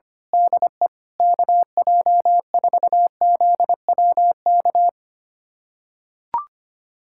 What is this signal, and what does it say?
Morse code, heard as a keyed tone, about 25 wpm, 700 Hz. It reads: DE KJ4ZWK
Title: USA Callsign DE Callsign - 6 Characters - 25wpm